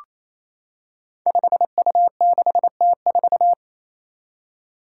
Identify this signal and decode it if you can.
Morse code, heard as a keyed tone, about 28 wpm, 700 Hz. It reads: 5U6T4
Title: Five Challenging Letter-Number Combinations 28wpm